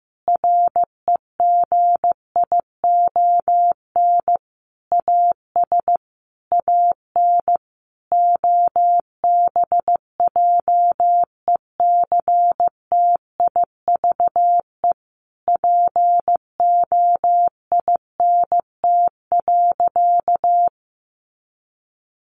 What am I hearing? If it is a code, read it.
Morse code, heard as a keyed tone, about 15 wpm, 700 Hz. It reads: REGION AS AN OBJECTIVE POINT.
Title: Sentences from Top 1500 Words 15wpm